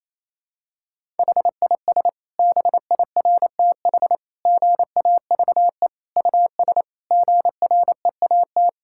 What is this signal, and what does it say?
Morse code, heard as a keyed tone, about 28 wpm, 700 Hz. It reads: HIS BIRTH GAVE US GREAT
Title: Sentences from Top 1000 Words - Review Part 5 - 28wpm